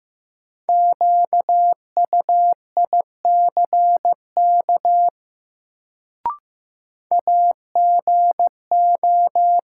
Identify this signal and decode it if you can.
Morse code, heard as a keyed tone, about 16 wpm, 700 Hz. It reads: QUICK AGO
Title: Sentences from Top 500 Words - New Word Drill - 15wpm